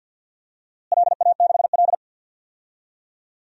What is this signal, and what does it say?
Morse code, heard as a keyed tone, about 49 wpm, 700 Hz. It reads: LA6L
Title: DXCC-100 Call Signs - 4 Characters - Repeated 3x - 50wpm